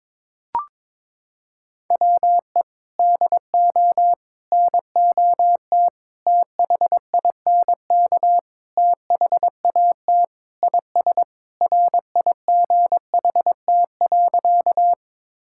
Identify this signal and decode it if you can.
Morse code, heard as a keyed tone, about 22 wpm, 700 Hz. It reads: WE DO NOT THINK THAT IS RIGHT.
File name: practice / 22wpm / Sentences with 4-Word Phrases - 22wpm